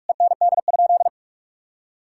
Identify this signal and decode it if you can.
Morse code, heard as a keyed tone, about 46 wpm, 700 Hz. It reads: END?
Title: Sentences from Top 1000 Words - Review Part 5 - 45wpm